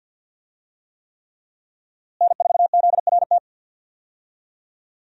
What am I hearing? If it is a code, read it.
Morse code, heard as a keyed tone, about 46 wpm, 700 Hz. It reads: N4ZRT
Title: USA Call Signs - 5 Characters - Repeated 3x - 50wpm